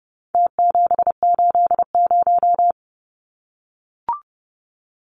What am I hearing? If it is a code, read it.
Morse code, heard as a keyed tone, about 30 wpm, 700 Hz. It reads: T780
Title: Numbers - Contest Style - 4-Digit 30wpm